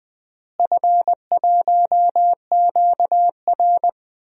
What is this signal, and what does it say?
Morse code, heard as a keyed tone, about 20 wpm, 700 Hz. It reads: F1QR
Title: DXCC-100 Call Signs - 4 Characters - Repeated 2x - 20wpm